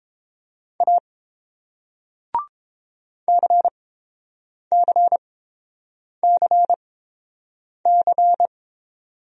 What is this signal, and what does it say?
Morse code, heard as a keyed tone, about 31 wpm, spaced as for 8 wpm, 700 Hz. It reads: A CCCC